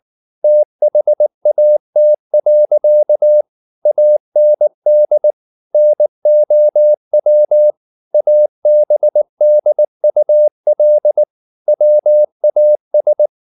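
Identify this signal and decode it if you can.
Morse code, heard as a keyed tone, about 19 wpm, 600 Hz. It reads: THAT. AND NOW ABDUL WAS
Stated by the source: Morse code at 19 wpm from CW med Gnister, reading Ken Follett's novel Never in English.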